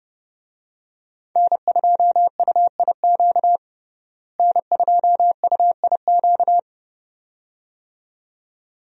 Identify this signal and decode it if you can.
Morse code, heard as a keyed tone, about 32 wpm, 700 Hz. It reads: N2UIQ N2UIQ
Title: USA Call Signs - 5 Characters - Repeated 2x - 30wpm